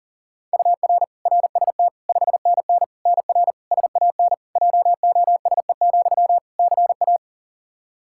Morse code, heard as a keyed tone, about 37 wpm, 700 Hz. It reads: UR RST 5NN NR SAN JOSE, CA
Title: Practice QSO - Exchange 1 - Beginning - 40wpm